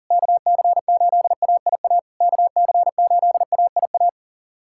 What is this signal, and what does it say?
Morse code, heard as a keyed tone, about 40 wpm, 690 Hz. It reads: KC8AIA KC8AIA
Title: SOTA QSO Practice 40wpm